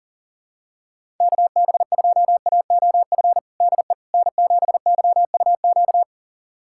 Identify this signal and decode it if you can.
Morse code, heard as a keyed tone, about 40 wpm, 700 Hz. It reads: KB2AOF DE N7YUQ
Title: USA Callsign DE Callsign 40wpm